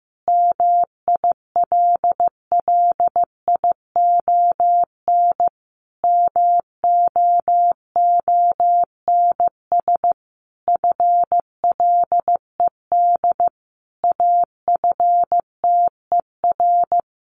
Morse code, heard as a keyed tone, about 15 wpm, 700 Hz. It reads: MILLION MOONS FLED AFTER
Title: A Sound Of Thunder 15wpm